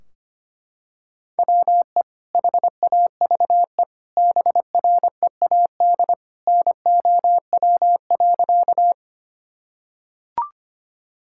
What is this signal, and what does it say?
Morse code, heard as a keyed tone, about 25 wpm, 700 Hz. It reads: WE HAVE BREAD NOW.
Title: Sentences from Top 1600 Words 25wpm